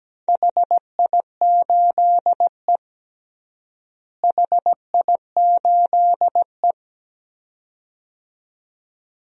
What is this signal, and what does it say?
Morse code, heard as a keyed tone, about 17 wpm, 700 Hz. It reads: HI8E HI8E
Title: DXCC-100 Call Signs - 4 Characters - Repeated 2x - 17wpm